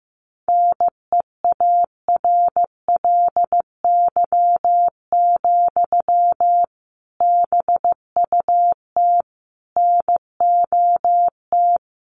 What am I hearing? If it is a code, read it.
Morse code, heard as a keyed tone, about 15 wpm, 700 Hz. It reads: NEARLY, BUT NOT
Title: Sentences from Top 1300 Words 15wpm